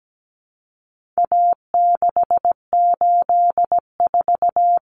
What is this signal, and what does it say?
Morse code, heard as a keyed tone, about 17 wpm, 700 Hz. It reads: A684
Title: Common Cut Numbers - 4-Digit 17wpm